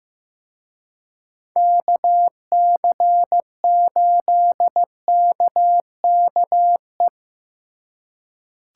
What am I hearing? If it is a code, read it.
Morse code, heard as a keyed tone, about 18 wpm, 700 Hz. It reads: KC8KKE
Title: Speed Racing - USA 2x3 Call Signs 15wpm